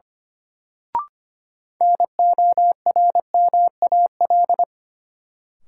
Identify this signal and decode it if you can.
Morse code, heard as a keyed tone, about 25 wpm, 700 Hz. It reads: NORMAL
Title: Sentences from Top 1200 Words - New Word Drill Alt - 25wpm